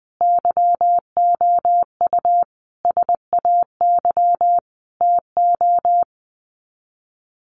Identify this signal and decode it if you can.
Morse code, heard as a keyed tone, about 20 wpm, 700 Hz. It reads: YOU SAY TO
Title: Sets of 5 Words 20wpm